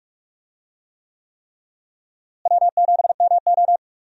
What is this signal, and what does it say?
Morse code, heard as a keyed tone, about 46 wpm, 700 Hz. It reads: W7MO
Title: USA Call Signs - 4 Characters - Repeated 3x - 45wpm